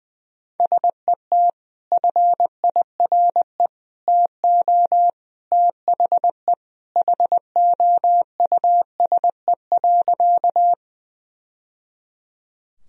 Morse code, heard as a keyed tone, about 20 wpm, 700 Hz. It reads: SET FIRE TO THE HOUSE.
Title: Sentences from Top 1000 Words - Review Part 1 - 20wpm